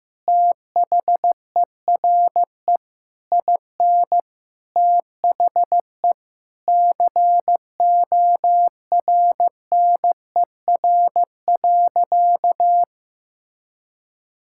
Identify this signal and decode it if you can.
Morse code, heard as a keyed tone, about 15 wpm, 700 Hz. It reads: THERE IN THE CORNER.
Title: Sentences from Top 1600 Words 15wpm